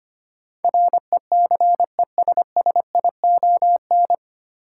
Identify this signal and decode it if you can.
Morse code, heard as a keyed tone, about 25 wpm, 690 Hz. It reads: RECESSION
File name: practice / 25wpm / 9 Letter Words - First 3 Letters Given - 25wpm